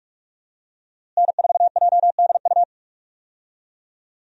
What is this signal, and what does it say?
Morse code, heard as a keyed tone, about 47 wpm, 700 Hz. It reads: N4JDU
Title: USA Call Signs - 5 Characters - Repeated 2x - 45wpm